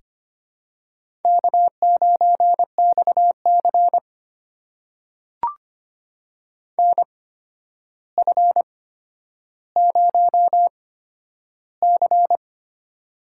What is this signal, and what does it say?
Morse code, heard as a keyed tone, about 25 wpm, spaced as for 11 wpm, 700 Hz. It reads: K9XC NF0C
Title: USA Call Signs - 4 Characters - 8x Character Spacing - 25wpm